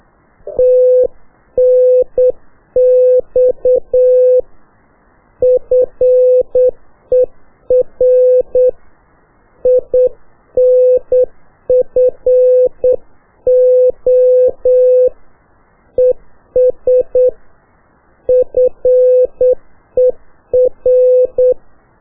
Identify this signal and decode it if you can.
Morse code, heard as a keyed tone, about 8 wpm, 500 Hz. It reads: TNX FER INFO ES FER
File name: qso1-qrs